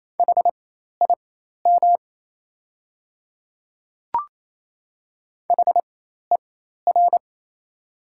Morse code, heard as a keyed tone, about 27 wpm, spaced as for 13 wpm, 690 Hz. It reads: HIM HER
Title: 4x Character Spacing - Top 100 Words - 28wpm